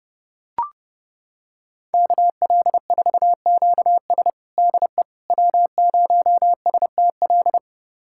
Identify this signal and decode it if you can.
Morse code, heard as a keyed tone, about 30 wpm, 700 Hz. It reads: KL4QS DE W0STL
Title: USA Callsign DE Callsign - 5 Characters - 30wpm